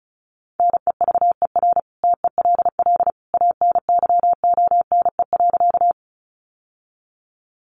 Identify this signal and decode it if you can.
Morse code, heard as a keyed tone, about 34 wpm, 700 Hz. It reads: NEVER TELL ANYONE.
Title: Sentences from Top 900 Words 35wpm